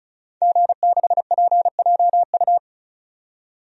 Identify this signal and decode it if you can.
Morse code, heard as a keyed tone, about 34 wpm, 700 Hz. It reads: G6PJU
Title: DXCC-100 Call Signs - 5 Characters - 35wpm